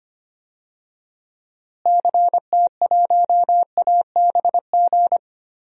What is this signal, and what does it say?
Morse code, heard as a keyed tone, about 25 wpm, 700 Hz. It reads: CT1ABG
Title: DXCC-100 Call Signs - 6 Characters - Repeated 2x - 25wpm